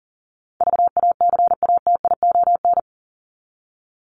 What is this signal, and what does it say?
Morse code, heard as a keyed tone, about 40 wpm, 700 Hz. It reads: VACATION
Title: Top 1501-1750 Words 40wpm